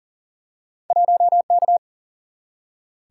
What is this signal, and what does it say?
Morse code, heard as a keyed tone, about 39 wpm, 700 Hz. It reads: 1K
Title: Single-Letter Single-Number 40wpm